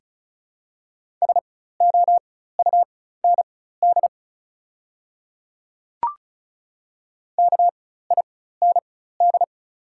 Morse code, heard as a keyed tone, about 36 wpm, spaced as for 16 wpm, 700 Hz. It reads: SOUND KIND
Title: 4x Character Spacing - Top 300 Words - 35wpm